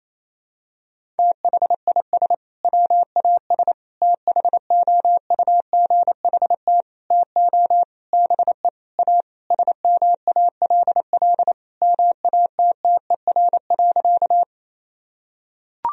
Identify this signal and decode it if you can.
Morse code, heard as a keyed tone, about 28 wpm, 700 Hz. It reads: THIS WAS THOUGHT TO BE A SMALL MATTER.
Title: Sentences from Top 1000 Words - Review Part 1 - 28wpm